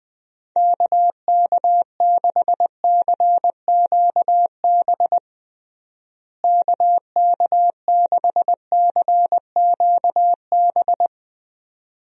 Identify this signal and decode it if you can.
Morse code, heard as a keyed tone, about 20 wpm, 700 Hz. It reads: KK6CQB KK6CQB
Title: USA Call Signs - 6 Characters - Repeated 2x - 20wpm